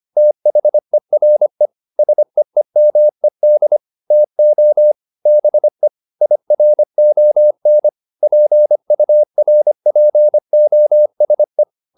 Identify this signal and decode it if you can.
Morse code, heard as a keyed tone, about 25 wpm, 600 Hz. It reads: THERE SEEMED TO BE IRON PURPOSE